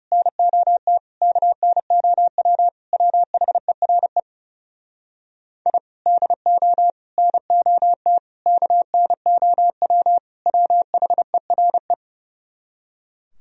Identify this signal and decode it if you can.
Morse code, heard as a keyed tone, about 40 wpm, 700 Hz. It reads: NOT KNOW WHERE I DO NOT KNOW WHERE